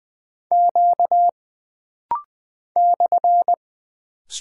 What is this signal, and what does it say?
Morse code, heard as a keyed tone, about 20 wpm, 700 Hz. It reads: Q /
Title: Single Character (Letter:Number:Punctuation:Prosign) - Rapid-Fire - 20wpm